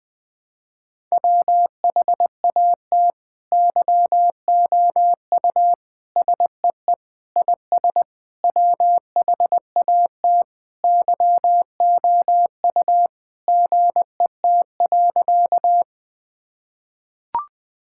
Morse code, heard as a keyed tone, about 21 wpm, 700 Hz. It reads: WHAT YOU SEE IS WHAT YOU GET.
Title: Sentences with 7-Word Phrases - 20wpm